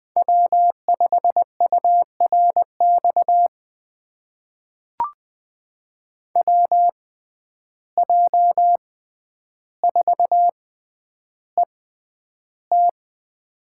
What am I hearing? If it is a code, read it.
Morse code, heard as a keyed tone, about 20 wpm, spaced as for 10 wpm, 700 Hz. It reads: W5URX WJ4ET
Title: USA Call Signs - 5 Characters - 6x Character Spacing - 20wpm